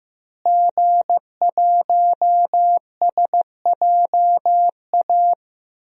Morse code, heard as a keyed tone, about 15 wpm, 700 Hz. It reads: G1SJA
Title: DXCC-100 Call Signs - 5 Characters - Repeated 2x - 15wpm